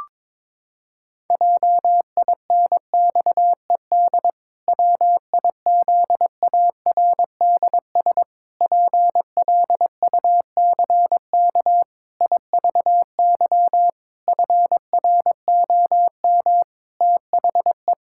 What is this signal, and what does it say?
Morse code, heard as a keyed tone, about 22 wpm, 700 Hz. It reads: JINXED WIZARDS PLUCK IVY FROM THE
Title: Pangram Challenge v2 - 22wpm